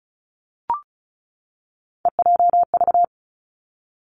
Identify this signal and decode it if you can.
Morse code, heard as a keyed tone, about 36 wpm, 700 Hz. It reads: EJV
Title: Three Challenging Letter-Number Combinations 35wpm